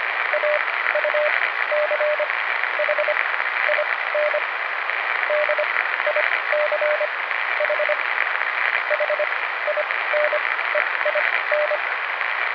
Morse code, heard as a keyed tone, about 25 wpm, spaced as for 15 wpm, 600 Hz, with radio noise. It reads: AUCH IN DICH HINEIN